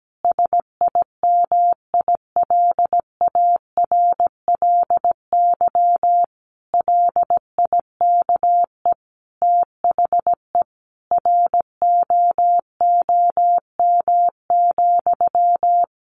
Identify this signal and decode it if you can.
Morse code, heard as a keyed tone, about 17 wpm, 700 Hz. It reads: SIMILARLY LIKE THE ROOM,
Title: Sentences from Top 1400 Words 17wpm